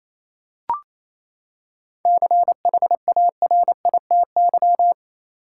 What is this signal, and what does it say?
Morse code, heard as a keyed tone, about 28 wpm, 700 Hz. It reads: CHARITY
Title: Seven Letter Words - 28wpm